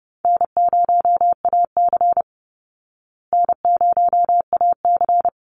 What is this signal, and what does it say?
Morse code, heard as a keyed tone, about 30 wpm, 700 Hz. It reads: N0AC N0AC
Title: Call Signs Repeated 3x - 30wpm